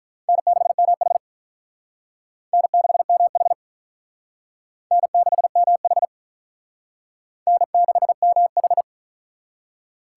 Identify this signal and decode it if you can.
Morse code, heard as a keyed tone, about 43 wpm, 700 Hz. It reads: N6MH N6MH N6MH N6MH
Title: Speed Racing - USA 1x2 Call Signs 35wpm